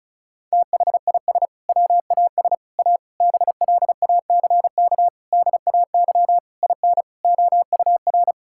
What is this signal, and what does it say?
Morse code, heard as a keyed tone, about 36 wpm, 700 Hz. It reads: THIS WAS A BLACK DAY IN OUR
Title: Sentences from Top 1600 Words 35wpm